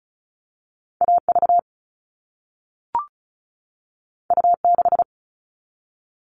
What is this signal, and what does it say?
Morse code, heard as a keyed tone, about 36 wpm, 700 Hz. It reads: AV U6
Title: All Cut Numbers - 2-Digit 35wpm